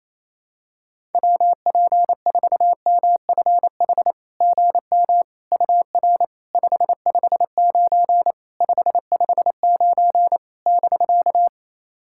Morse code, heard as a keyed tone, about 27 wpm, 660 Hz. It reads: WP4MFH GM UR 559 559 <BK>
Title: SOTA QSO Practice 28wpm